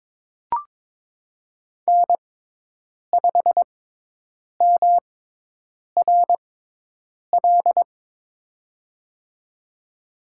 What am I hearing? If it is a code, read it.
Morse code, heard as a keyed tone, about 22 wpm, spaced as for 8 wpm, 700 Hz. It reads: N5MRL
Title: USA Call Signs - 6x Character Spacing - 22wpm